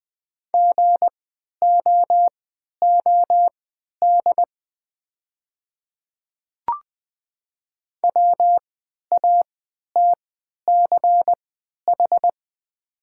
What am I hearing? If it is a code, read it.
Morse code, heard as a keyed tone, about 21 wpm, spaced as for 11 wpm, 700 Hz. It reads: GOOD WATCH
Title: 3x Character Spacing - Top 300 Words - 20wpm